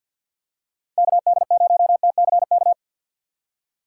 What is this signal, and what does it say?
Morse code, heard as a keyed tone, about 53 wpm, 700 Hz. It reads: KD0TCK
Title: USA Call Signs - 6 Characters - 8x Character Spacing - 50wpm